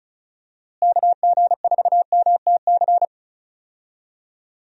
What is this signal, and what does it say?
Morse code, heard as a keyed tone, about 34 wpm, 700 Hz. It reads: KG4MTC
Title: USA Call Signs - 6 Characters - Repeated 3x - 35wpm